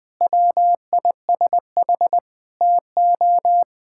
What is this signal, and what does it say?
Morse code, heard as a keyed tone, about 20 wpm, 700 Hz. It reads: WISH TO
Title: Sentences with 7-Word Phrases - 20wpm